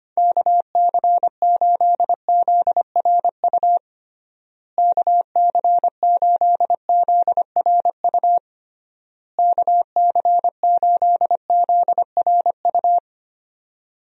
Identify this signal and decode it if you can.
Morse code, heard as a keyed tone, about 25 wpm, 700 Hz. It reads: KC8ZRU KC8ZRU KC8ZRU
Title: USA Call Signs - 6 Characters - Repeated 3x - 25wpm